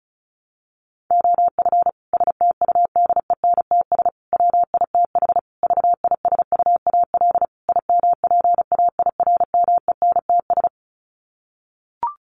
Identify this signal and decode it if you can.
Morse code, heard as a keyed tone, about 34 wpm, 700 Hz. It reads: OF STUDENTS WITH VISUAL IMPAIRMENTS
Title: Sets of 5 Words Encore 35wpm